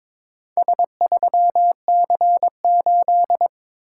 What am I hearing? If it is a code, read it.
Morse code, heard as a keyed tone, about 22 wpm, 690 Hz. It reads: S3C8
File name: practice / 22wpm / Four Letter-Number Combinations - 22wpm